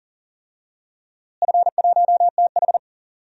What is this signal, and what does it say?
Morse code, heard as a keyed tone, about 37 wpm, 700 Hz. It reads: F1TH
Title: DXCC-100 Call Signs - 4 Characters - Repeated 3x - 40wpm